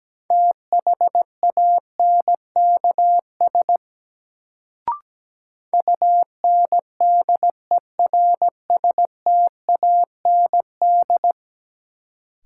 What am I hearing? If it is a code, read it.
Morse code, heard as a keyed tone, about 17 wpm, 700 Hz. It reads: THANKS UNDERSTAND
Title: Sentences from Top 600 Words - New Word Drill - 17wpm